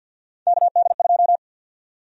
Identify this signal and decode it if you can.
Morse code, heard as a keyed tone, about 50 wpm, 700 Hz. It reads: KD2
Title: USA Call Sign Prefixes - 50wpm